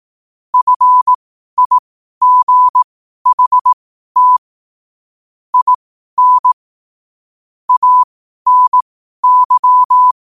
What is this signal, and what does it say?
Morse code, heard as a keyed tone, about 18 wpm, spaced as for 12 wpm, 1000 Hz. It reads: FIGHT IN ANY
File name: qotdfiles / 12wpm